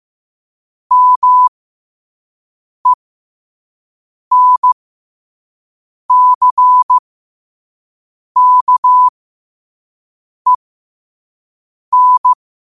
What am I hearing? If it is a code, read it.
Morse code, heard as a keyed tone, about 15 wpm, spaced as for 5 wpm, 1000 Hz. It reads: MENCKEN
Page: Quotes for Fri, 15 Aug 2025 in Morse Code at 5 words per minute.